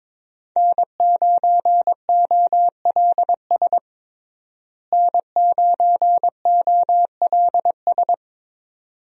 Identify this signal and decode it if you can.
Morse code, heard as a keyed tone, about 22 wpm, 700 Hz. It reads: N9OLS N9OLS
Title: USA Call Signs - 5 Characters - Repeated 2x - 22wpm